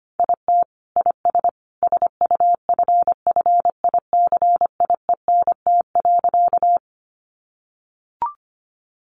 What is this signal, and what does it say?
Morse code, heard as a keyed tone, about 25 wpm, 700 Hz. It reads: IT IS SUFFICIENT.
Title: Sentences from Top 1400 Words 25wpm